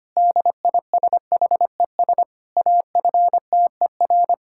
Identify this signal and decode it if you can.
Morse code, heard as a keyed tone, about 25 wpm, 700 Hz. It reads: DISHES AFTER
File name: practice / 25wpm / Difficult Sentences with Hints - 25wpm